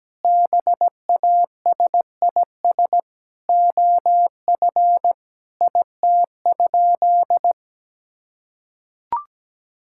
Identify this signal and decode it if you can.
Morse code, heard as a keyed tone, about 17 wpm, 700 Hz. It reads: BASIS OF IT?
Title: Sentences from Top 1000 Words - Review Part 5 - 17wpm